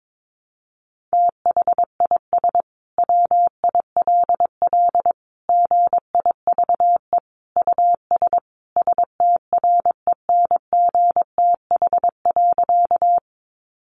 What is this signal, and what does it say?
Morse code, heard as a keyed tone, about 22 wpm, 700 Hz. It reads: THIS WILL GIVE US STRENGTH.
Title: Sentences from Top 1300 Words 22wpm